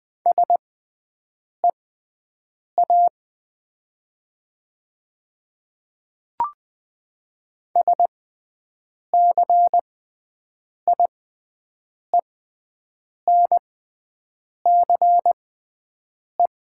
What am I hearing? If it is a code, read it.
Morse code, heard as a keyed tone, about 20 wpm, spaced as for 7 wpm, 700 Hz. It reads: SEA SCIENCE
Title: 6x Character Spacing - Top 500 Words - 20wpm